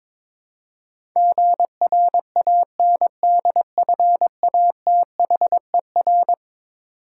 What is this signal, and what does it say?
Morse code, heard as a keyed tone, about 22 wpm, 700 Hz. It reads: GRANDFATHER
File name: practice / 22wpm / Eleven Letter Words - 22wpm